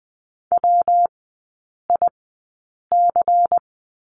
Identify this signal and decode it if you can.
Morse code, heard as a keyed tone, about 20 wpm, spaced as for 8 wpm, 690 Hz. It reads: WIC
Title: 205 Lesson 33e - Letter j - Three Characters 20wpm